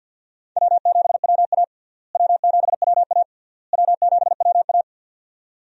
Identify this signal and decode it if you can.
Morse code, heard as a keyed tone, about 49 wpm, 700 Hz. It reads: W7WA W7WA W7WA
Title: USA Call Signs - 4 Characters - Repeated 3x - 50wpm